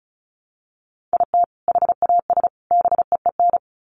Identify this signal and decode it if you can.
Morse code, heard as a keyed tone, about 34 wpm, 700 Hz. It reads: IT HAS BEEN
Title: Sets of 3 Words 35wpm